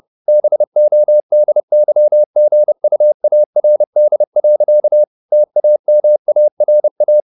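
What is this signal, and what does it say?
Morse code, heard as a keyed tone, about 30 wpm, 600 Hz. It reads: BODYGUARD. TAMARA
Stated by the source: Morse code at 30 wpm, the novel Never by Ken Follett in English, from CW med Gnister.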